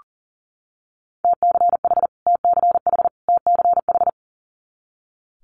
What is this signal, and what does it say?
Morse code, heard as a keyed tone, about 38 wpm, 690 Hz. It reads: TCH TCH TCH
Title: Multi-letter Phonemes Repeated 3x 40wpm